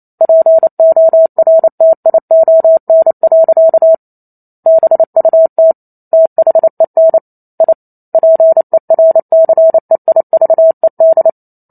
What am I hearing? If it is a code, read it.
Morse code, heard as a keyed tone, about 28 wpm, 650 Hz. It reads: PORTION. BUT THEN I PERCEIVED